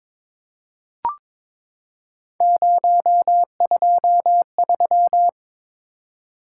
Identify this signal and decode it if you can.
Morse code, heard as a keyed tone, about 22 wpm, 700 Hz. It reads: 023
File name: practice / 22wpm / Three Letter-Number Combinations - 22wpm